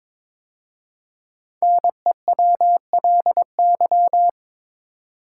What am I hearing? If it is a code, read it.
Morse code, heard as a keyed tone, about 23 wpm, 700 Hz. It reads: NEWLY
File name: 5 Letter Words - First 3 Letters Given - 22wpm